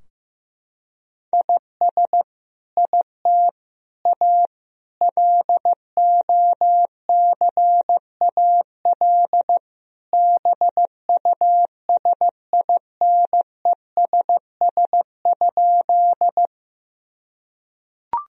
Morse code, heard as a keyed tone, about 15 wpm, 700 Hz. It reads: IS IT A LOCAL BUSINESS?
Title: Sentences from Top 1000 Words - Review Part 3 - 15wpm